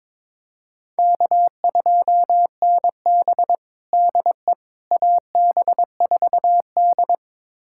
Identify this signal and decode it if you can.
Morse code, heard as a keyed tone, about 21 wpm, 700 Hz. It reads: K2NB DE AB4D
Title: USA Callsign DE Callsign - 4 Characters - 22wpm